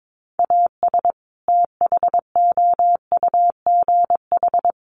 Morse code, heard as a keyed tone, about 22 wpm, 700 Hz. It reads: AS THOUGH
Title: Sentences from Top 1500 Words 22wpm